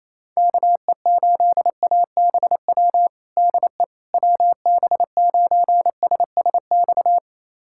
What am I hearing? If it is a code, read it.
Morse code, heard as a keyed tone, about 28 wpm, 700 Hz. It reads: KE8ABW DE WB9SSX
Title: USA Callsign DE Callsign - 6 Characters - 28wpm